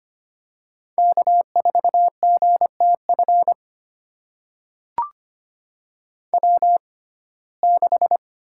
Morse code, heard as a keyed tone, about 25 wpm, spaced as for 18 wpm, 700 Hz. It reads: K4GTF W6
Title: USA Call Signs - 5 Characters - 6x Character Spacing - 25wpm